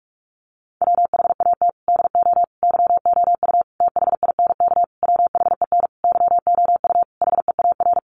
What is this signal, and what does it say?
Morse code, heard as a keyed tone, about 44 wpm, 700 Hz. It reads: WHAT DO YOU THINK WHEN YOU HEAR
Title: Sentences from Top 300 Words 45wpm